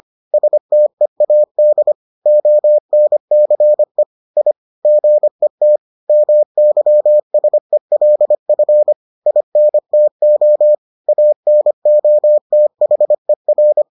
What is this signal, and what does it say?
Morse code, heard as a keyed tone, about 25 wpm, 600 Hz. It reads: STEAD ONCE I GET MYSELF INTO ANOTHER